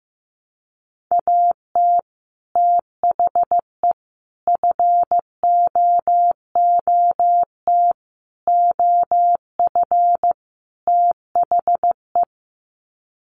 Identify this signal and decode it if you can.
Morse code, heard as a keyed tone, about 15 wpm, 700 Hz. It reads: AT THE FOOT OF THE
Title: Sets of 5 Words 15wpm